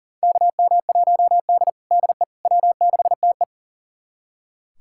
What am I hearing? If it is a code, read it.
Morse code, heard as a keyed tone, about 40 wpm, 700 Hz. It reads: KM1D DE W6TE
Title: USA Callsign DE Callsign - 4 Characters - 40wpm